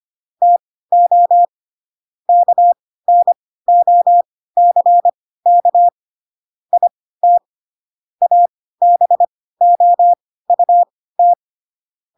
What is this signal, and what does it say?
Morse code, heard as a keyed tone, about 25 wpm, spaced as for 16 wpm, 700 Hz. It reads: TO KNOCK IT ABOUT
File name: Pinocchio / Pinocchio ch1 10 25 ik1hge